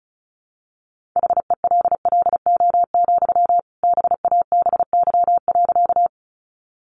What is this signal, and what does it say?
Morse code, heard as a keyed tone, about 34 wpm, 700 Hz. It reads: HELLO, BABY.
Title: Sentences from Top 1300 Words 35wpm